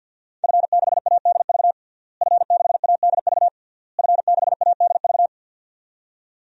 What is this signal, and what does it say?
Morse code, heard as a keyed tone, about 52 wpm, 700 Hz. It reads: F6ADV F6ADV F6ADV
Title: DXCC-100 Call Signs - 5 Characters - Repeated 3x - 50wpm